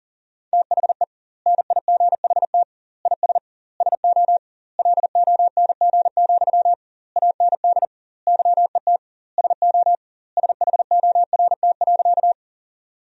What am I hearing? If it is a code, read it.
Morse code, heard as a keyed tone, about 40 wpm, 700 Hz. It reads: THE NIGHT IS SO LONG, AND YET SO SHORT.
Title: 2x Word-Spacing - Sentences from Top 500 Words - 40wpm